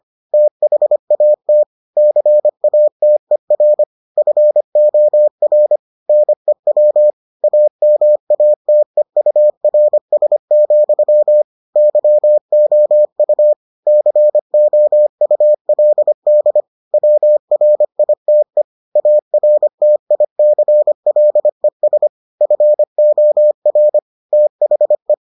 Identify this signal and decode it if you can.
Morse code, heard as a keyed tone, about 25 wpm, 600 Hz. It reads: THAT CATER FOR NEW AMATEURS, YOU COULD WRITE ARTICLES FOR THE